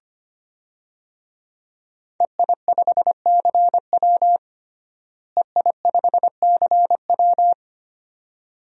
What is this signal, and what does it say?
Morse code, heard as a keyed tone, about 25 wpm, 700 Hz. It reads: EI5CW EI5CW
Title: DXCC-100 Call Signs - 5 Characters - Repeated 2x - 25wpm